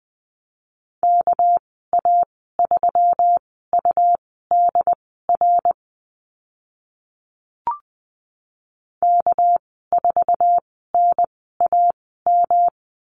Morse code, heard as a keyed tone, about 20 wpm, spaced as for 14 wpm, 700 Hz. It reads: KA3UDR K4NAM
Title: 219 Lesson 35g - Number 0 - Call Signs 20wpm